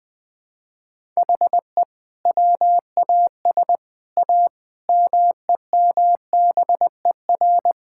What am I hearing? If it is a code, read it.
Morse code, heard as a keyed tone, about 20 wpm, 700 Hz. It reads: HE WAS A MEMBER
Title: Sets of 6 Words - 20wpm